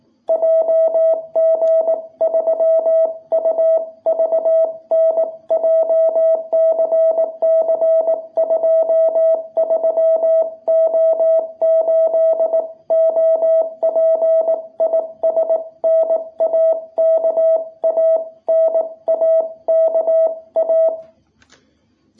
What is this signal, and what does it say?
Morse code, heard as a keyed tone, about 18 wpm, 660 Hz. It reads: JG3UVN JCC23O8 OP IS NAKA NAKA
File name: jh8cbh / 2021cw